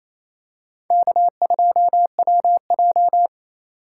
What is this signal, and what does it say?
Morse code, heard as a keyed tone, about 28 wpm, 700 Hz. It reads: K2WJ
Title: USA Call Signs - 4 Characters - 2x Character Spacing - 28wpm